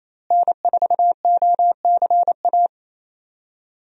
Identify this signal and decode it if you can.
Morse code, heard as a keyed tone, about 27 wpm, 700 Hz. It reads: N4OCA
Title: USA Call Signs - 5 Characters - Repeated 3x - 28wpm